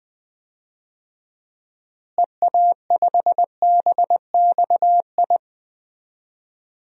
Morse code, heard as a keyed tone, about 20 wpm, 700 Hz. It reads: EA5BXI
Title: DXCC-100 Call Signs - 6 Characters - Repeated 3x - 20wpm